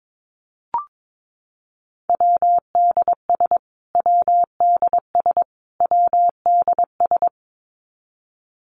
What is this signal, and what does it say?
Morse code, heard as a keyed tone, about 22 wpm, 700 Hz. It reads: WDS WDS WDS
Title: Abbreviations Repeated 3x - 22wpm